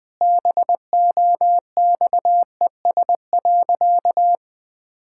Morse code, heard as a keyed tone, about 20 wpm, 700 Hz. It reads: BOXES.
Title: Pangram Challenge v2 - 20wpm